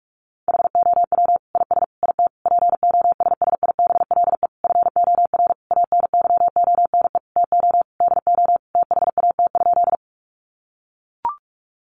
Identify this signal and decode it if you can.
Morse code, heard as a keyed tone, about 44 wpm, 700 Hz. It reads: HOW IS IT POSSIBLE FOR ANYONE TO DO THAT?
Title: Sentences from Top 900 Words 45wpm